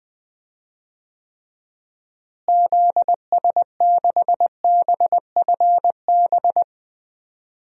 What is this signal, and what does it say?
Morse code, heard as a keyed tone, about 20 wpm, 700 Hz. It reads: ZS6BFB
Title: DXCC-100 Call Signs - 6 Characters - Repeated 2x - 20wpm